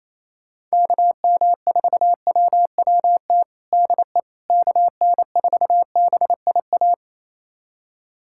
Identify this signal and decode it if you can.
Morse code, heard as a keyed tone, about 27 wpm, 700 Hz. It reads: KM4WWT DE KN4BIA
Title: USA Callsign DE Callsign 28wpm